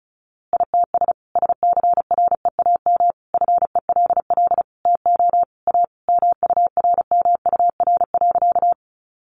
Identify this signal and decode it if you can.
Morse code, heard as a keyed tone, about 34 wpm, 700 Hz. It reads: ITS SCREAM FELL TO A MURMUR.
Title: A Sound Of Thunder 35wpm